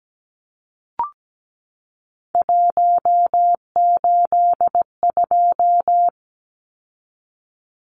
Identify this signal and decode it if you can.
Morse code, heard as a keyed tone, about 17 wpm, 700 Hz. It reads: 182
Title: Numbers - 3-Digit 17wpm